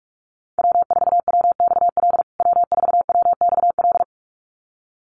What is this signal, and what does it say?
Morse code, heard as a keyed tone, about 44 wpm, 700 Hz. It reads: W4WXL W4WXL
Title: USA Call Signs Repeated 2x 45wpm